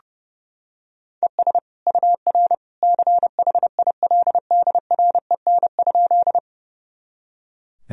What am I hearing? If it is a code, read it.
Morse code, heard as a keyed tone, about 30 wpm, 700 Hz. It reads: ES UR CHILDREN?
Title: Sentences from Top 300 Words and Abbreviations 30wpm